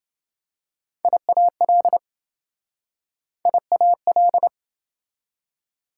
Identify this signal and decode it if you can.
Morse code, heard as a keyed tone, about 28 wpm, 700 Hz. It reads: IAL IAL